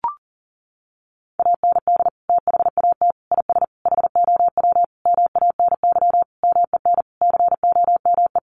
Morse code, heard as a keyed tone, about 40 wpm, 700 Hz. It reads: AND THAT IS HOW MANY MEN COME
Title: Sentences from Top 1400 Words 40wpm